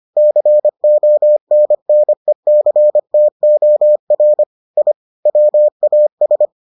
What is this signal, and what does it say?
Morse code, heard as a keyed tone, about 25 wpm, 600 Hz. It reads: CONNECTOR I WAS